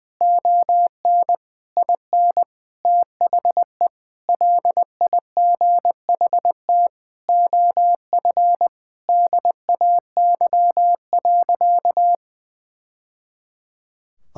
Morse code, heard as a keyed tone, about 20 wpm, 700 Hz. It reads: ON IN THE LIGHT OF DAY.
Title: Sentences from Top 1000 Words - Review Part 5 - 20wpm